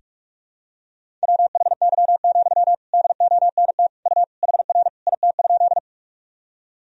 Morse code, heard as a keyed tone, about 46 wpm, 700 Hz. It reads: WHY, DONT U HR IT?
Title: Sentences from Top 300 Words and Abbreviations 45wpm